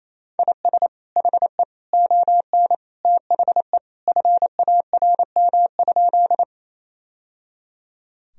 Morse code, heard as a keyed tone, about 27 wpm, 700 Hz. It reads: IS HE ON THE FARM?
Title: Sentences from Top 1000 Words - Review Part 2 - 28wpm